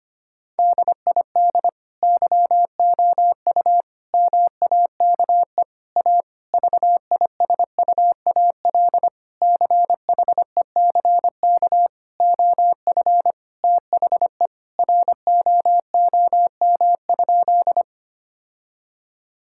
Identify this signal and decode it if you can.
Morse code, heard as a keyed tone, about 25 wpm, 700 Hz. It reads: DID YOU MAKE A VISUAL CHECK OF THE ROOM?
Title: Sentences from Top 1000 Words - Review Part 6 - 25wpm